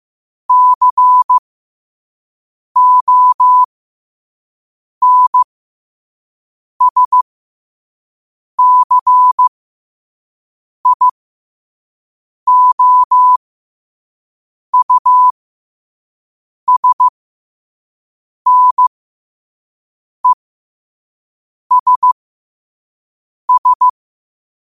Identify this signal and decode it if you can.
Morse code, heard as a keyed tone, about 15 wpm, spaced as for 5 wpm, 1000 Hz. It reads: CONSCIOUSNESS
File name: qotdfiles / 5wpm